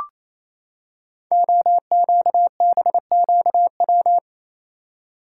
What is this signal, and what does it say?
Morse code, heard as a keyed tone, about 27 wpm, 690 Hz. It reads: OQBQW
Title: Code Groups - 5 Letters - 28wpm